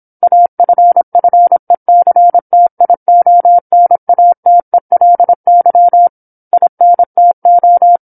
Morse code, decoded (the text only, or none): AFFECTIONATELY INTO